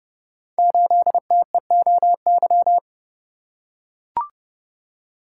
Morse code, heard as a keyed tone, about 30 wpm, 700 Hz. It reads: 8TEOY
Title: Code Groups - 5 Letter:Numbers - 30wpm